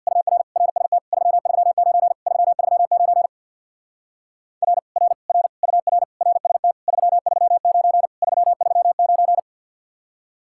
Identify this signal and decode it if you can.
Morse code, heard as a keyed tone, about 62 wpm, 700 Hz. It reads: UR RST 339 339 R R R UR RST 339 339
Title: Speed-Racing Common QSO Phrases - 50wpm